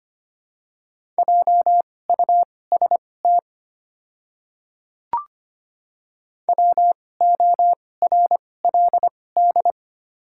Morse code, heard as a keyed tone, about 25 wpm, spaced as for 18 wpm, 690 Hz. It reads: JUST WORLD
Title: 2x Character Spacing - Top 200 Words - 25wpm